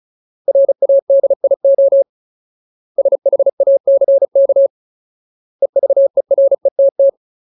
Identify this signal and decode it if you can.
Morse code, heard as a keyed tone, about 34 wpm, 550 Hz. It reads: RADIO SHACK EVERETT
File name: Ultimate Word List / 35WPM Ultimate